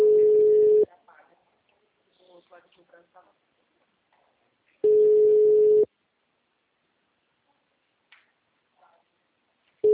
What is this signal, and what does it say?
Ringback tone (Europe, 425 Hz), with speech.